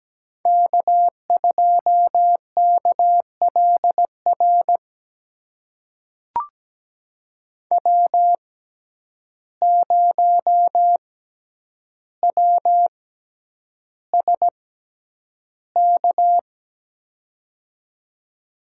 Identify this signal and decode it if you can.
Morse code, heard as a keyed tone, about 17 wpm, spaced as for 9 wpm, 700 Hz. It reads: K2KLR W0WSK
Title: USA Call Signs - 5 Characters - 6x Character Spacing - 17wpm